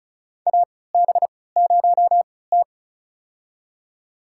Morse code, heard as a keyed tone, about 35 wpm, spaced as for 20 wpm, 700 Hz. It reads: AB0T
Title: USA Call Signs - 3x Character Spacing - 35wpm